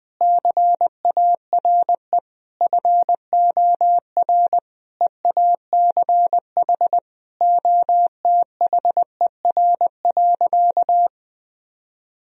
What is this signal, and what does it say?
Morse code, heard as a keyed tone, about 20 wpm, 700 Hz. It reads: CARE FOR EACH OTHER.
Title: Sentences from Top 1000 Words - Review Part 3 - 20wpm